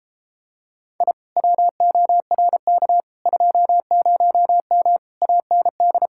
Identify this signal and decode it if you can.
Morse code, heard as a keyed tone, about 29 wpm, 700 Hz. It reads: I WORK 20M AND